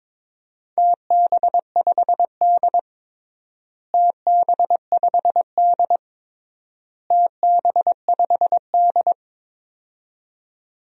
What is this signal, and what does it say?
Morse code, heard as a keyed tone, about 22 wpm, 700 Hz. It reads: TB5D TB5D TB5D
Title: DXCC-100 Call Signs - 4 Characters - Repeated 3x - 22wpm